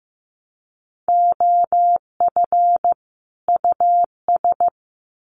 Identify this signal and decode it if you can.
Morse code, heard as a keyed tone, about 15 wpm, 700 Hz. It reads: OF US
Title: Sets of 2 Words 15wpm